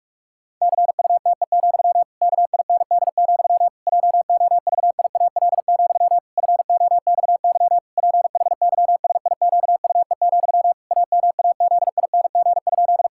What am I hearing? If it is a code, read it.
Morse code, heard as a keyed tone, about 46 wpm, 700 Hz. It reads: CUTE, KIND, JOVIAL, FOXY PHYSIQUE, AMAZING?
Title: Pangram Challenge v2 - 45wpm